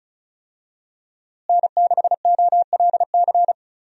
Morse code, heard as a keyed tone, about 34 wpm, 700 Hz. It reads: N6OLC
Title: USA Call Signs - 5 Characters - Repeated 2x - 35wpm